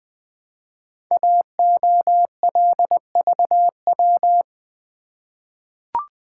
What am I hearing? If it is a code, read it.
Morse code, heard as a keyed tone, about 20 wpm, 700 Hz. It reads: AOLVW
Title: Code Groups - 5 Letters - 20wpm